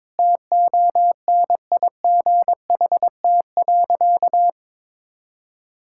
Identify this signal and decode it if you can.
Morse code, heard as a keyed tone, about 22 wpm, 700 Hz. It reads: TONIGHT.
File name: practice / 22wpm / Sentences with 2-Word Phrases - 22wpm